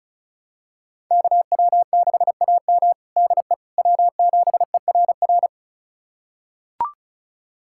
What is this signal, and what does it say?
Morse code, heard as a keyed tone, about 34 wpm, 700 Hz. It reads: KW6AM DE W7ERR
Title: USA Callsign DE Callsign - 5 Characters - 35wpm